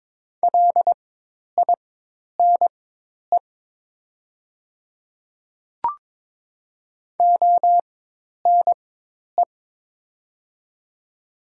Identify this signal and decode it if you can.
Morse code, heard as a keyed tone, about 22 wpm, spaced as for 10 wpm, 700 Hz. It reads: LINE ONE
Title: 4x Character Spacing - Top 200 Words - 22wpm